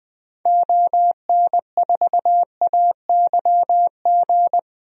Morse code, heard as a keyed tone, about 20 wpm, 700 Hz. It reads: ON4AYG
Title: DXCC-100 Call Signs - 6 Characters - Repeated 2x - 20wpm